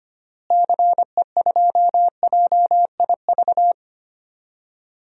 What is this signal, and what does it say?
Morse code, heard as a keyed tone, about 25 wpm, 700 Hz. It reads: CE2JIV
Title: DXCC-100 Call Signs - 6 Characters - Repeated 3x - 25wpm